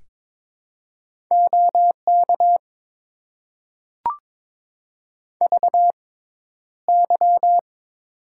Morse code, heard as a keyed tone, about 22 wpm, spaced as for 11 wpm, 700 Hz. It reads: OK VY
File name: Top 90 QSO Elements - 6x Character Spacing - 22wpm